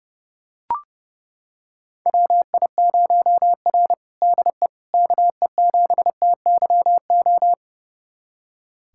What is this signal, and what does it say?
Morse code, heard as a keyed tone, about 30 wpm, 700 Hz. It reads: WI0R DE KE7TYO
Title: USA Callsign DE Callsign 30wpm